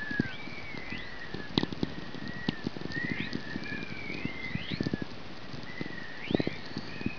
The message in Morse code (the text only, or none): none